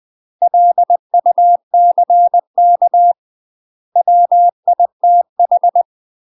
Morse code, heard as a keyed tone, about 20 wpm, 700 Hz. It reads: LUCK WITH
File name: UK MASTER QSOs TRAINING / 20WPM / UK MASTER QSOs 0071